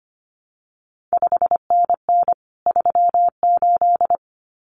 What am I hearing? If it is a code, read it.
Morse code, heard as a keyed tone, about 25 wpm, 700 Hz. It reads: 5NN 38
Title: CQ WW DX Contest Exchange Encore - 25wpm